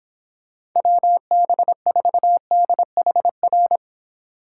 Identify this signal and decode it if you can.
Morse code, heard as a keyed tone, about 24 wpm, 700 Hz. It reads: WB4DHR